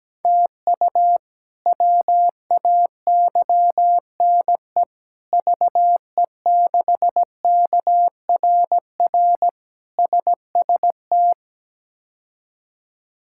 Morse code, heard as a keyed tone, about 17 wpm, 700 Hz. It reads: TU WAYNE VE6KRR SST
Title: K1USN SST QSO Practice - 17wpm